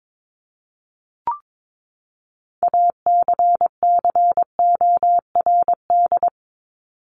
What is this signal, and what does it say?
Morse code, heard as a keyed tone, about 22 wpm, 700 Hz. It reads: ACCORD
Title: ING Suffix - Top 100 Words with Base Words 22wpm